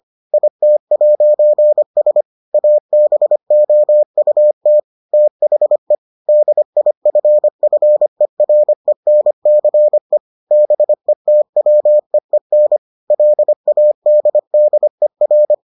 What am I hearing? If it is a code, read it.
Morse code, heard as a keyed tone, about 25 wpm, 600 Hz. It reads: IT'S ABOUT THE DIFFERENCE BETWEEN LADDER